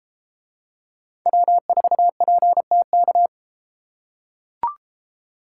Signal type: Morse code

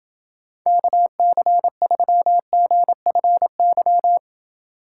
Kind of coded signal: Morse code

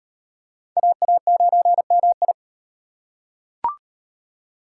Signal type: Morse code

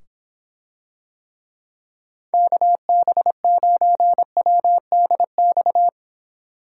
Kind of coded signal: Morse code

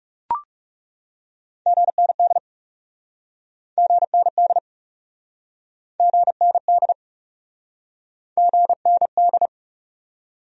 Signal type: Morse code

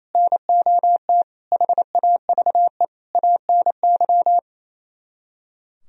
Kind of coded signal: Morse code

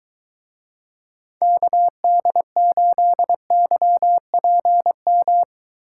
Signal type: Morse code

